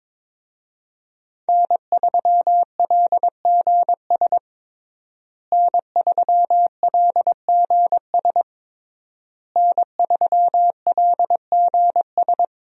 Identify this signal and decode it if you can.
Morse code, heard as a keyed tone, about 22 wpm, 700 Hz. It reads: N3LGS N3LGS N3LGS
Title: USA Call Signs - 5 Characters - Repeated 3x - 22wpm